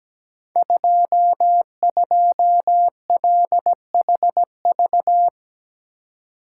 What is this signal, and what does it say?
Morse code, heard as a keyed tone, about 17 wpm, 700 Hz. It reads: 22LHV
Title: Code Groups - 5 Letter:Numbers - 17wpm